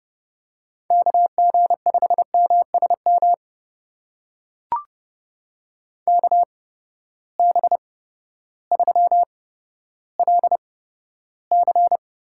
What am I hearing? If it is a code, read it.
Morse code, heard as a keyed tone, about 32 wpm, spaced as for 14 wpm, 700 Hz. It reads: KG5MSM KB3LC
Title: USA Call Signs - 6 Characters - 8x Character Spacing - 30wpm